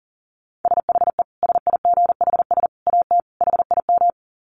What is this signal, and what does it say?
Morse code, heard as a keyed tone, about 40 wpm, 700 Hz. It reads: SHE SIGHS AT HIM
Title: Dit Challenge 40wpm